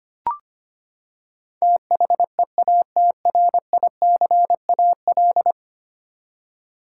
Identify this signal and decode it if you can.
Morse code, heard as a keyed tone, about 25 wpm, 700 Hz. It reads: THEATRICAL
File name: practice / 25wpm / Ten Letter Words - 25wpm